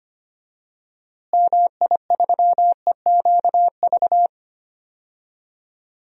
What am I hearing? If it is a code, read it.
Morse code, heard as a keyed tone, about 25 wpm, 700 Hz. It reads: MI3EQV
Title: DXCC-100 Call Signs - 6 Characters - Repeated 3x - 25wpm